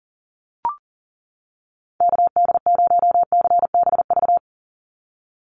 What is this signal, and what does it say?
Morse code, heard as a keyed tone, about 38 wpm, 700 Hz. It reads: KD0CBV
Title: USA Call Signs 40wpm